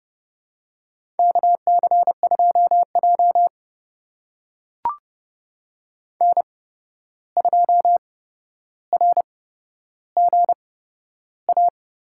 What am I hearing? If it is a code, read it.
Morse code, heard as a keyed tone, about 32 wpm, spaced as for 12 wpm, 700 Hz. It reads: KC2J N2RGA
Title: USA Call Signs - 8x Character Spacing - 30wpm